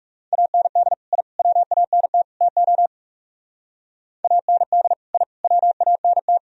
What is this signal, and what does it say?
Morse code, heard as a keyed tone, about 37 wpm, 700 Hz. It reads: AND I WANT TO AND I WANT
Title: Speed-Racing - Sets of 4 Words 30wpm